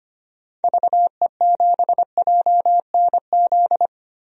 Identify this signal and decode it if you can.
Morse code, heard as a keyed tone, about 25 wpm, 700 Hz. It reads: VE7JNZ
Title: DXCC-100 Call Signs - 6 Characters - Repeated 2x - 25wpm